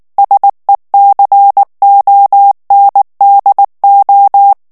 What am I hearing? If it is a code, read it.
Morse code, heard as a keyed tone, about 19 wpm, 800 Hz. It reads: SECONDO
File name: morse mp3 / ITALIEN texte clair 2 vitesse 18 mots minute